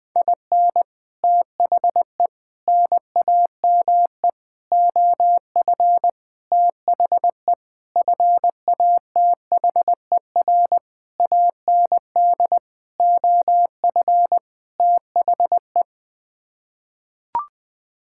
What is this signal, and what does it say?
Morse code, heard as a keyed tone, about 20 wpm, 700 Hz. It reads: IN THE NAME OF THE FATHER AND OF THE
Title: Sets of 9 Words - 20wpm